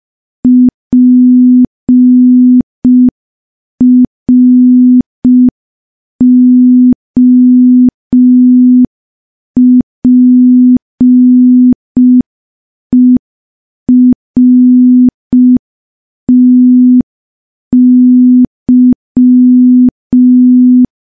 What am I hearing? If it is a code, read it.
Morse code, heard as a keyed tone, about 5 wpm, 250 Hz. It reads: PROPERTY
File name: morse / morse-training-05WPM